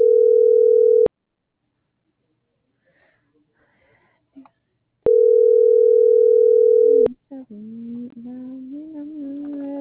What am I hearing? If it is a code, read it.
Ringback tone (North America), 440 + 480 Hz.
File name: CENTRAL TERRESTRE / out karis - .5912